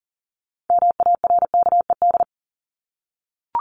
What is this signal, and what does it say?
Morse code, heard as a keyed tone, about 40 wpm, 700 Hz. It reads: MARKED
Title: ED Suffix - Top 100 Words with Base Words 40wpm